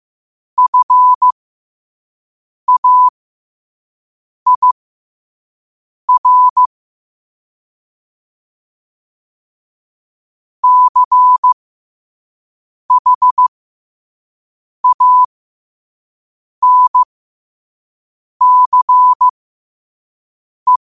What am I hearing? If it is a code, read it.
Morse code, heard as a keyed tone, about 15 wpm, spaced as for 5 wpm, 1000 Hz. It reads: FAIR CHANCE
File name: qotdfiles / 5wpm